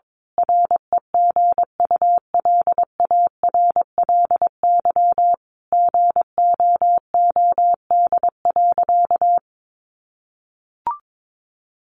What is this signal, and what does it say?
Morse code, heard as a keyed tone, about 22 wpm, 700 Hz. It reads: REGULARLY GOOD.
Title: Sentences from Top 1400 Words 22wpm